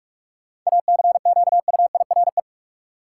Morse code, heard as a keyed tone, about 46 wpm, 700 Hz. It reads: ACQUIRE
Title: Seven Letter Words - 45wpm